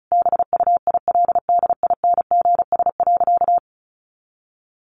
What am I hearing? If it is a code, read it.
Morse code, heard as a keyed tone, about 34 wpm, 700 Hz. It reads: BUILDINGS.
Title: Easy Phrases 35wpm